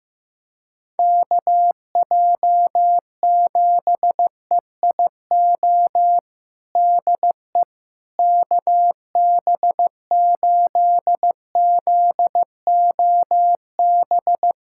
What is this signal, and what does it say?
Morse code, heard as a keyed tone, about 15 wpm, 700 Hz. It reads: KJ7EIO DE KB8ZOB
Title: USA Callsign DE Callsign 15wpm